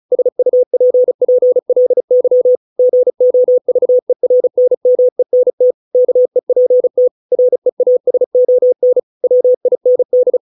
Morse code, heard as a keyed tone, about 37 wpm, 500 Hz. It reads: SUPPLY GOVERNMENT KEPT REASON WIND